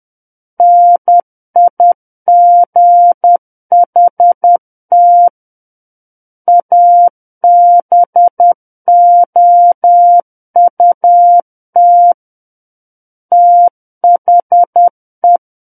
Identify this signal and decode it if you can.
Morse code, heard as a keyed tone, about 10 wpm, 700 Hz. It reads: NIGHT ABOUT THE
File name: Book One / 01-Chapter 1 - 10 WPM